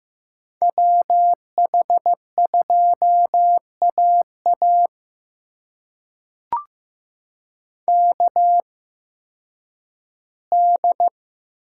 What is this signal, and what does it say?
Morse code, heard as a keyed tone, about 15 wpm, spaced as for 10 wpm, 700 Hz. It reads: WH2AA KD
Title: USA Call Signs - 8x Character Spacing - 15wpm